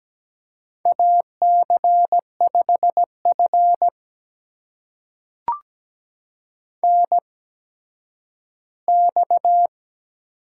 Morse code, heard as a keyed tone, about 17 wpm, spaced as for 10 wpm, 700 Hz. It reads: AC5F NX
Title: USA Call Signs - 4 Characters - 8x Character Spacing - 17wpm